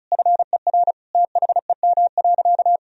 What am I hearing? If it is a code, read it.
Morse code, heard as a keyed tone, about 36 wpm, 700 Hz. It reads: FER THEM.
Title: Sentences from Top 300 Words and Abbreviations 35wpm